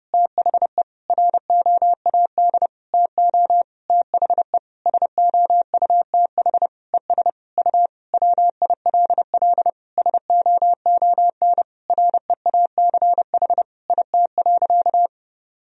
Morse code, heard as a keyed tone, about 30 wpm, 700 Hz. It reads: THE ROAD TO THE SOUTH ES U WILL SOON REACH IT.
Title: Sentences from Top 400 Words and Abbreviations 30wpm